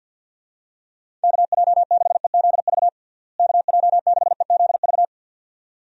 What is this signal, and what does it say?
Morse code, heard as a keyed tone, about 46 wpm, 700 Hz. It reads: KJ6EZV KJ6EZV
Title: USA Call Signs - 6 Characters - Repeated 2x - 50wpm